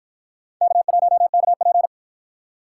Morse code, heard as a keyed tone, about 49 wpm, 700 Hz. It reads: K1KP